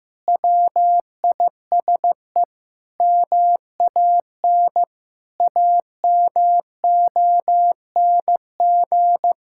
Morse code, heard as a keyed tone, about 16 wpm, 700 Hz. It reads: WISE MAN AMONG
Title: Sentences from Top 1600 Words 15wpm